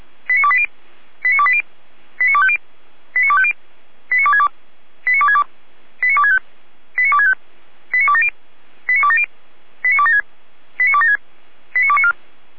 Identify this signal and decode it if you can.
Five-tone selective call (ZVEI), heard as quick runs of five tones, tones of about 70 ms. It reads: 89280 89280 89250 89250 89262 89262 89265 89265 89270 89270 89276 89276 89294